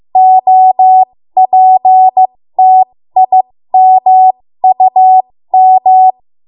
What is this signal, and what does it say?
Morse code, heard as a keyed tone, about 15 wpm, spaced as for 13 wpm, 750 Hz. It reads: OPTIMUM
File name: Morse / 13WPM